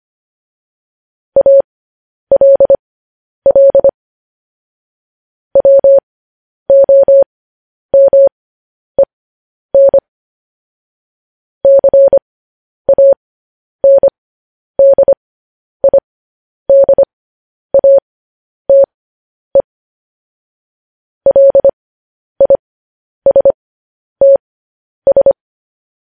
Morse code, heard as a keyed tone, about 25 wpm, spaced as for 10 wpm, 550 Hz. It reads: ALL WOMEN CANDIDATE LISTS
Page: news headlines in morse code at 10 wpm